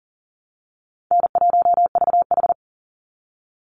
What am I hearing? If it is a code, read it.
Morse code, heard as a keyed tone, about 40 wpm, 700 Hz. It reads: N1VH
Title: Easy Call Signs Repeated 3x - 40wpm